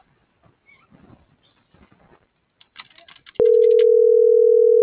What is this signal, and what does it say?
Ringback tone (North America), 440 + 480 Hz.